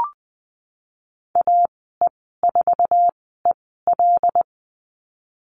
Morse code, heard as a keyed tone, about 19 wpm, spaced as for 14 wpm, 700 Hz. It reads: AE4EL
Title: 122 Lesson 20g - Number 3 - Call Signs 20wpm